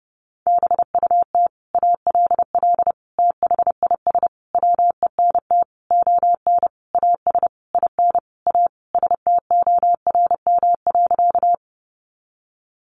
Morse code, heard as a keyed tone, about 30 wpm, 700 Hz. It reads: BUT ALL THIS WENT ON AS IN A STORM.
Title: Sentences from Top 1500 Words 30wpm